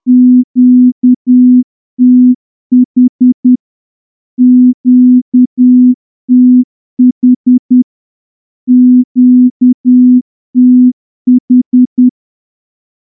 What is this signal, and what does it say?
Morse code, heard as a keyed tone, about 10 wpm, 250 Hz. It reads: QTH QTH QTH
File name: morse-training-10WPM